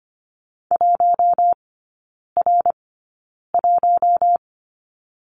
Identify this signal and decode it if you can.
Morse code, heard as a keyed tone, about 25 wpm, spaced as for 9 wpm, 690 Hz. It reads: 1R1
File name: Lesson 14e - Number 2 - Three Characters 25wpm